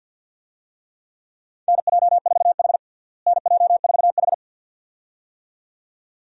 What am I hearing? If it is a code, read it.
Morse code, heard as a keyed tone, about 53 wpm, 700 Hz. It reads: NJ4H NJ4H
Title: USA Call Signs - 4 Characters - Repeated 2x - 50wpm